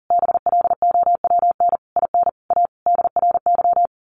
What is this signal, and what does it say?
Morse code, heard as a keyed tone, about 40 wpm, 700 Hz. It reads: BLOWN IN A DRY
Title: A Sound Of Thunder 40wpm